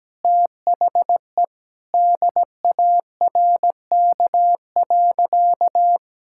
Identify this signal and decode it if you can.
Morse code, heard as a keyed tone, about 17 wpm, 700 Hz. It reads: THE DARK.
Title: Sentences from Top 1600 Words 17wpm